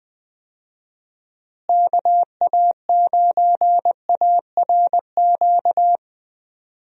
Morse code, heard as a keyed tone, about 20 wpm, 700 Hz. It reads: KA9ARQ
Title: USA Call Signs - 6 Characters - Repeated 2x - 20wpm